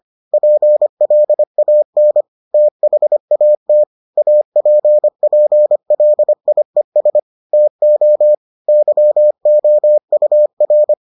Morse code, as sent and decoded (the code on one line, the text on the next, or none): .--. .-.. .- -. / - .... .- - / .- .--. .--. .-.. .. ... / - --- / -.-- --- ..- .-.
PLAN THAT APPLIES TO YOUR